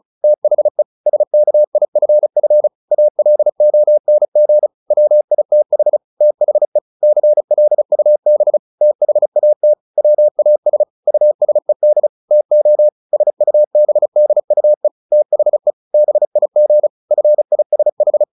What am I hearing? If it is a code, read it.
Morse code, heard as a keyed tone, about 34 wpm, 600 Hz. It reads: THE SKIFF ALONG WITH THE CLUB THAT WAS USED TO SUBDUE THE BIG FISH
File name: the old man and the sea / the old man and the sea 1 35wpm